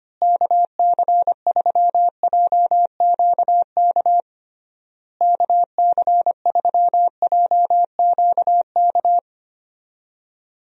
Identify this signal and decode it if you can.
Morse code, heard as a keyed tone, about 25 wpm, 700 Hz. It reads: KC3JQK KC3JQK
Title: USA Call Signs - 6 Characters - Repeated 3x - 25wpm